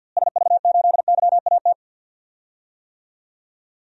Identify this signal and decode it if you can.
Morse code, heard as a keyed tone, about 53 wpm, 700 Hz. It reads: SV8YAT
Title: DXCC-100 Call Signs - 6 Characters - Repeated 2x - 50wpm